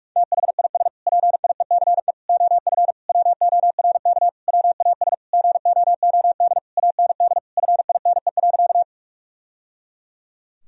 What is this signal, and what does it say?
Morse code, heard as a keyed tone, about 46 wpm, 700 Hz. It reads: THIS PIECE OF WORK WAS GOOD AND FINE.
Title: Sentences from Top 1000 Words - Review Part 2 - 45wpm